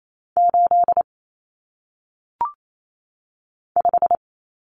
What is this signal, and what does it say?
Morse code, heard as a keyed tone, about 28 wpm, 700 Hz. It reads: 8 5
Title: Numbers - 1-Digit 28wpm